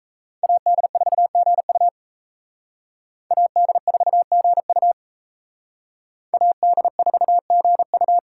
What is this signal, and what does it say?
Morse code, heard as a keyed tone, about 37 wpm, 700 Hz. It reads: AD4GU AD4GU AD4GU